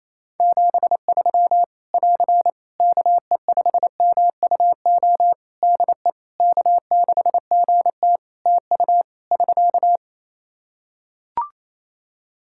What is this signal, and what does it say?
Morse code, heard as a keyed tone, about 27 wpm, 700 Hz. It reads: 73 <AR> KE5MUO DE K6GT TU <SK>
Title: Practice QSO - Exchange 4 - Ending - 28wpm